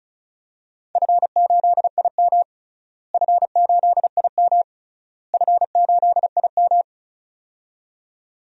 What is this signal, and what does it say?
Morse code, heard as a keyed tone, about 36 wpm, 700 Hz. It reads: F8IM F8IM F8IM
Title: DXCC-100 Call Signs - 4 Characters - Repeated 3x - 35wpm